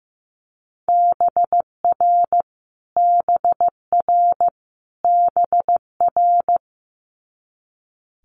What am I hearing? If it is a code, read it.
Morse code, heard as a keyed tone, about 15 wpm, 700 Hz. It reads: BR BR BR
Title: Multi-letter Phonemes Repeated 3x 15wpm